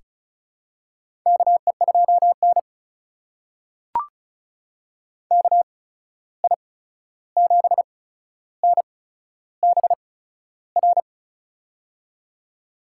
Morse code, heard as a keyed tone, about 36 wpm, spaced as for 13 wpm, 700 Hz. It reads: KE2N KI7NBR
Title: USA Call Signs - 8x Character Spacing - 35wpm